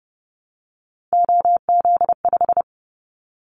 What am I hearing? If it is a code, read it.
Morse code, heard as a keyed tone, about 30 wpm, 700 Hz. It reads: OZ5
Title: Callsign Prefixes for Top 100 DXCC - 30wpm